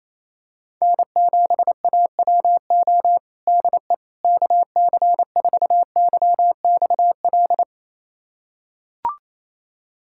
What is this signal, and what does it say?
Morse code, heard as a keyed tone, about 28 wpm, 700 Hz. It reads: N7AWO DE KC4YXL
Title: USA Callsign DE Callsign 28wpm